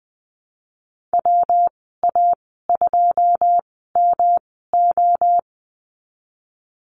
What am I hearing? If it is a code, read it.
Morse code, heard as a keyed tone, about 20 wpm, spaced as for 15 wpm, 700 Hz. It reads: WA2MO
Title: 140 Lesson 23g - Letter f - Call Signs 20wpm